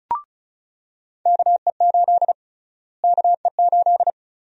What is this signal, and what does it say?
Morse code, heard as a keyed tone, about 36 wpm, 700 Hz. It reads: KE8 KE8
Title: USA Call Sign Prefixes - Repeated 2x - 35wpm